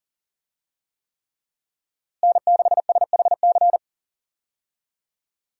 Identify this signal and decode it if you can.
Morse code, heard as a keyed tone, about 40 wpm, 700 Hz. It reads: N6SHC
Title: USA Call Signs - 5 Characters - Repeated 3x - 40wpm